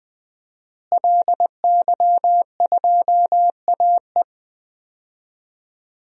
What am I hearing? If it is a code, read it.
Morse code, heard as a keyed tone, about 20 wpm, 700 Hz. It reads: LY2AE
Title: DXCC-100 Call Signs - 5 Characters - Repeated 2x - 20wpm